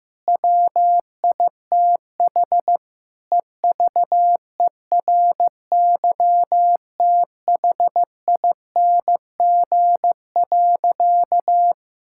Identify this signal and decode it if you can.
Morse code, heard as a keyed tone, about 15 wpm, 700 Hz. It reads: WITH EVERYTHING.
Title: Sentences from Top 1000 Words - Review Part 5 - 15wpm